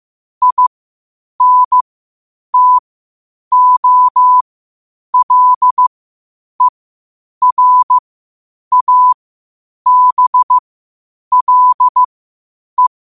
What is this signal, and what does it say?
Morse code, heard as a keyed tone, about 15 wpm, spaced as for 8 wpm, 1000 Hz. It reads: INTOLERABLE
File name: qotdfiles / 8wpm